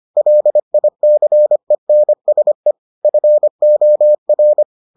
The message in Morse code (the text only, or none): LICENSE FOR